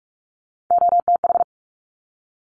Morse code, heard as a keyed tone, about 46 wpm, 700 Hz. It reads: OTH
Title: Top Three Letter Combinations in Words 45wpm